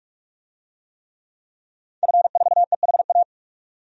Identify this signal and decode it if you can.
Morse code, heard as a keyed tone, about 42 wpm, 700 Hz. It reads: F4EHA
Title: DXCC-100 Call Signs - 5 Characters - 45wpm